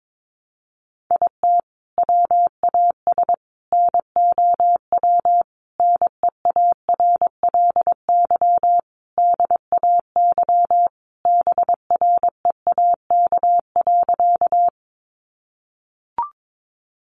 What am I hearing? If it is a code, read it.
Morse code, heard as a keyed tone, about 22 wpm, 700 Hz. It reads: IT WAS NOW NEARLY DAY BREAK.
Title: Sentences from Top 1300 Words 22wpm